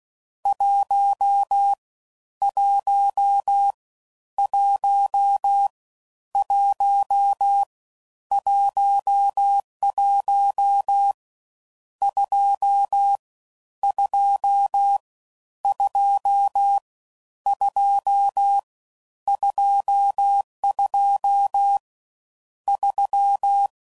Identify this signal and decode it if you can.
Morse code, heard as a keyed tone, about 16 wpm, 800 Hz. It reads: 1 1 1 1 11 2 2 2 2 22 3
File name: Mixed / Misc / Numbs